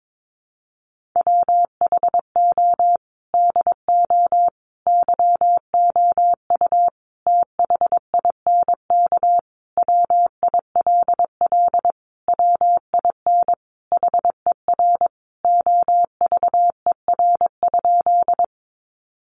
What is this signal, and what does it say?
Morse code, heard as a keyed tone, about 22 wpm, 700 Hz. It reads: WHO DO YOU THINK WILL WIN HER OVER?
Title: Sentences from Top 1500 Words 22wpm